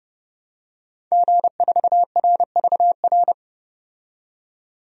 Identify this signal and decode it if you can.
Morse code, heard as a keyed tone, about 31 wpm, 700 Hz. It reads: G4RVR
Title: DXCC-100 Call Signs - 5 Characters - 30wpm